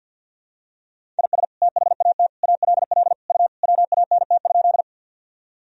Morse code, heard as a keyed tone, about 52 wpm, 700 Hz. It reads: IS THAT ALL U WANT?
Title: Sentences from Top 200 Words and Abbreviations 50wpm